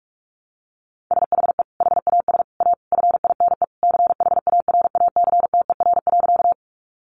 Morse code, heard as a keyed tone, about 44 wpm, 700 Hz. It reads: SHE HAS A FINE CHARACTER.
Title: Sentences from Top 1200 Words 45wpm